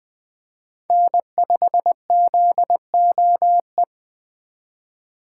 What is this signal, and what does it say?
Morse code, heard as a keyed tone, about 20 wpm, 700 Hz. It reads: N5ZOE